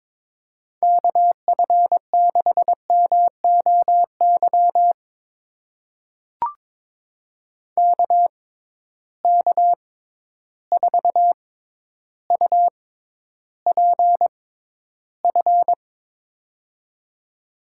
Morse code, heard as a keyed tone, about 22 wpm, spaced as for 11 wpm, 700 Hz. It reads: KF6MOY KK4UPF
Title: USA Call Signs - 6 Characters - 6x Character Spacing - 22wpm